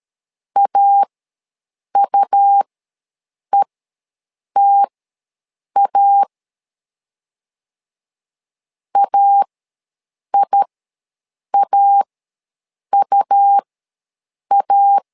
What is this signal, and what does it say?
Morse code, heard as a keyed tone, about 13 wpm, spaced as for 7 wpm, 800 Hz. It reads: AUETA AIAUA
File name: LEZIONI / 2 UHDN / 2 U ws 12wpm 1655